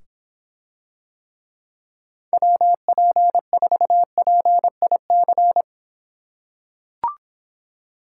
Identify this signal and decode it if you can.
Morse code, heard as a keyed tone, about 26 wpm, 700 Hz. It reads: WP4PIC